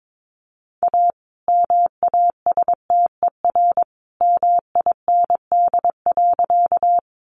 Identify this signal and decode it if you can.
Morse code, heard as a keyed tone, about 22 wpm, 700 Hz. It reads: A MASTER MIND.
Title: Sentences from Top 1300 Words 22wpm